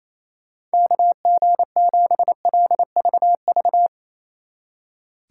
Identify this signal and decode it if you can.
Morse code, heard as a keyed tone, about 35 wpm, 700 Hz. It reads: KG7LVV
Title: Speed Racing - USA 2x3 Call Signs 28wpm